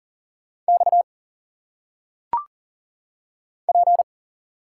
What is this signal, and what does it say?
Morse code, heard as a keyed tone, about 40 wpm, 700 Hz. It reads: X P
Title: Single Letters - Phonetic Pronunciation 40wpm